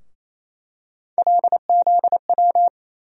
Morse code, heard as a keyed tone, about 27 wpm, 700 Hz. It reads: LZW
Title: USA Call Sign Postfixes - 3 Letters 28wpm